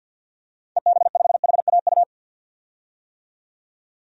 Morse code, heard as a keyed tone, about 51 wpm, 700 Hz. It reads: EB5HAU
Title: DXCC-100 Call Signs - 6 Characters - 50wpm